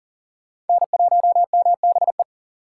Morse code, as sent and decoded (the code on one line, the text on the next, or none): -. .---- -- -... .
N1MBE